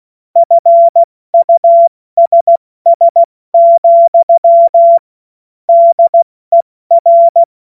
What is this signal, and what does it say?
Morse code, heard as a keyed tone, about 16 wpm, 680 Hz. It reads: FUSS, DER
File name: Witz-08-Juli-2025-70-F80